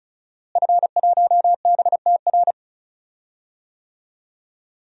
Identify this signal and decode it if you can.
Morse code, heard as a keyed tone, about 36 wpm, 700 Hz. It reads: F1BTR
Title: DXCC-100 Call Signs - 5 Characters - Repeated 3x - 35wpm